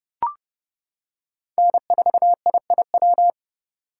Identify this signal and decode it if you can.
Morse code, heard as a keyed tone, about 28 wpm, 700 Hz. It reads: N4IIW